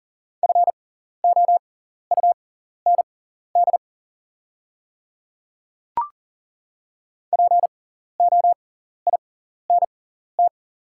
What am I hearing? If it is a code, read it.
Morse code, heard as a keyed tone, about 40 wpm, spaced as for 14 wpm, 700 Hz. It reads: FOUND POINT
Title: 6x Character Spacing - Top 300 Words - 40wpm